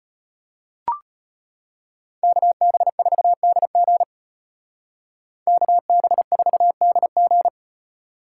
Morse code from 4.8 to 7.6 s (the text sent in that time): KB4DG